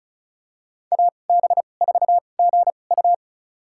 Morse code, heard as a keyed tone, about 36 wpm, spaced as for 25 wpm, 700 Hz. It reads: AB4GU
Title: USA Call Signs - 5 Characters - 2x Character Spacing - 35wpm